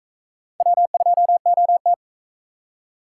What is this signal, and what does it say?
Morse code, heard as a keyed tone, about 40 wpm, 700 Hz. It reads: W2OT